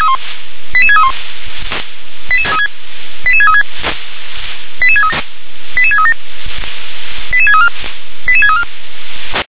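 Five-tone selective call (ZVEI), heard as quick runs of five tones, tones of about 70 ms. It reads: 88631 88636 88637 88634 88634
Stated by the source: ZVEI five-tone call 88730 received on an SDR.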